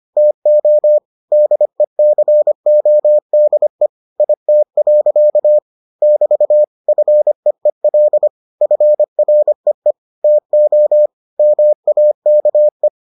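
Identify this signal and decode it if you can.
Morse code, heard as a keyed tone, about 25 wpm, 600 Hz. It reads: TO DECODE IT. <BT> FEEL FREE TO MAKE
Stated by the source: Morse edition of the Foundations of Amateur Radio episodes.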